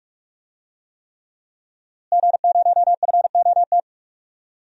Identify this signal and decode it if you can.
Morse code, heard as a keyed tone, about 41 wpm, 700 Hz. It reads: G0FOT
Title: DXCC-100 Call Signs - 5 Characters - Repeated 3x - 45wpm